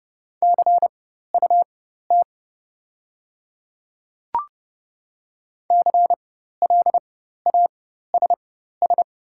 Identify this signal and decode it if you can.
Morse code, heard as a keyed tone, about 30 wpm, spaced as for 14 wpm, 690 Hz. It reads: CUT CLASS
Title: 4x Character Spacing - Top 500 Words - 30wpm